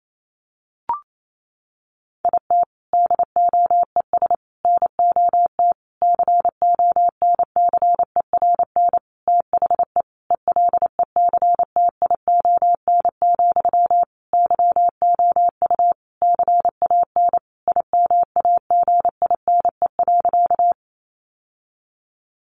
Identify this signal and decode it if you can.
Morse code, heard as a keyed tone, about 28 wpm, 700 Hz. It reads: IT DOES NOT CONCERN THE ELECTION, YOU CAN IMAGINE.
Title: Sentences from Top 1500 Words 28wpm